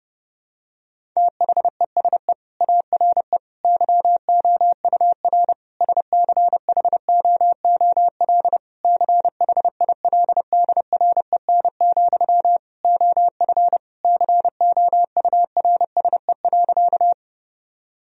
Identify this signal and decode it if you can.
Morse code, heard as a keyed tone, about 30 wpm, 700 Hz. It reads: THESE ARE YOUR SCHOOL CHILDREN, OF COURSE.
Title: Sentences from Top 1000 Words - Review Part 1 - 30wpm